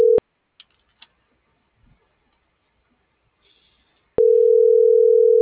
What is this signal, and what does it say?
Ringback tone (North America), 440 + 480 Hz.